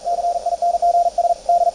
Morse code, heard as a keyed tone, about 41 wpm, 690 Hz. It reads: CINGS D
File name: attachments / CW 20m 200Hz